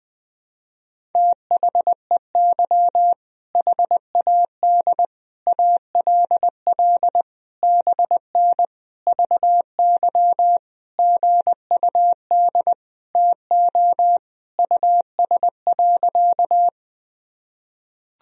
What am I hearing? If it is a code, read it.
Morse code, heard as a keyed tone, about 20 wpm, 700 Hz. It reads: THEY HAD ALL BN VY GUD TO US.
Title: Sentences from Top 200 Words and Abbreviations 20wpm